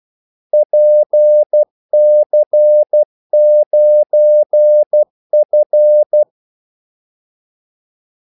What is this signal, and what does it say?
Morse code, heard as a keyed tone, about 12 wpm, 600 Hz. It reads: PC9F
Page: 2 ekstra ordmellemrum mellem kaldesignalerne